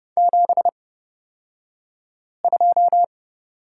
Morse code, heard as a keyed tone, about 30 wpm, spaced as for 5 wpm, 700 Hz. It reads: 72
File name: Lesson 38d - Letter q - Two Characters 30wpm